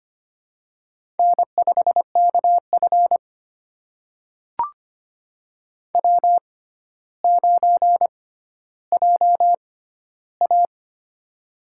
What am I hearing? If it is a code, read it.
Morse code, heard as a keyed tone, about 25 wpm, spaced as for 13 wpm, 700 Hz. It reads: N5KF W9JA
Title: USA Call Signs - 4 Characters - 6x Character Spacing - 25wpm